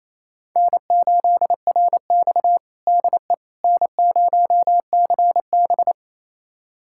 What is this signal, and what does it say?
Morse code, heard as a keyed tone, about 28 wpm, 700 Hz. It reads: N8RX DE N0CB
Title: USA Callsign DE Callsign - 4 Characters - 28wpm